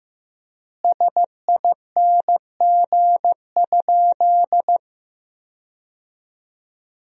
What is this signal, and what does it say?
Morse code, heard as a keyed tone, about 16 wpm, 700 Hz. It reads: SING?
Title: 4x Word-Spacing - Sentences from Top 400 Words - 15wpm